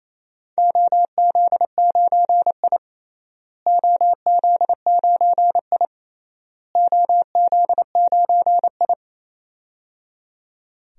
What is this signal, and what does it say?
Morse code, heard as a keyed tone, about 28 wpm, 700 Hz. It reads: OZ9I OZ9I OZ9I
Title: DXCC-100 Call Signs - 4 Characters - Repeated 3x - 28wpm